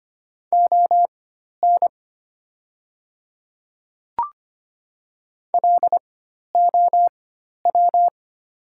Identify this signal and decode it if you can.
Morse code, heard as a keyed tone, about 25 wpm, spaced as for 12 wpm, 700 Hz. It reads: ON LOW
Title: 4x Character Spacing - Top 300 Words - 25wpm